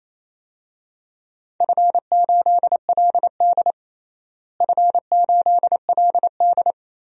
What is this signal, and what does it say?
Morse code, heard as a keyed tone, about 28 wpm, 700 Hz. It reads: F8LD F8LD
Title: DXCC-100 Call Signs - 4 Characters - Repeated 3x - 28wpm